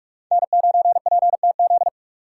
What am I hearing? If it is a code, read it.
Morse code, heard as a keyed tone, about 43 wpm, 700 Hz. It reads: N9PTZ